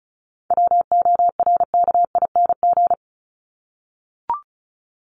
Morse code, heard as a keyed tone, about 34 wpm, 700 Hz. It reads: WORKING
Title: ING Suffix - Top 100 Words with Base Words 35wpm